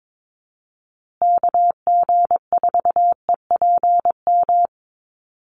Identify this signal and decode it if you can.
Morse code, heard as a keyed tone, about 22 wpm, 700 Hz. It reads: KG4EPM
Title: USA Call Signs Repeated 2x 22wpm